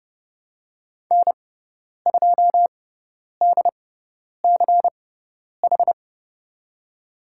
Morse code, heard as a keyed tone, about 31 wpm, spaced as for 10 wpm, 700 Hz. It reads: N2DCH
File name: Lesson 16g - Letter c - Call Signs 30wpm